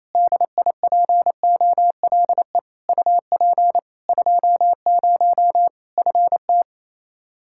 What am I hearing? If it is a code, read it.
Morse code, heard as a keyed tone, about 28 wpm, 700 Hz. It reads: DIPOLE UP 20 FT
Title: Practice QSO - Exchange 2 - Info - 28wpm